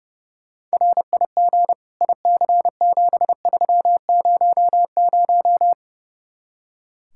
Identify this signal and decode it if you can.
Morse code, heard as a keyed tone, about 35 wpm, 700 Hz. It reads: RIG IC7300
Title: Speed-Racing Common QSO Phrases Encore - 30wpm